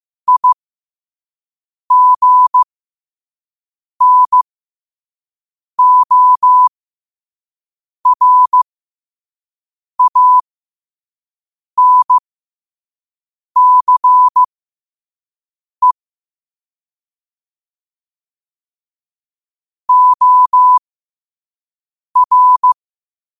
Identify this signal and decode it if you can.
Morse code, heard as a keyed tone, about 15 wpm, spaced as for 5 wpm, 1000 Hz. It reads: IGNORANCE OR
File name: qotdfiles / 5wpm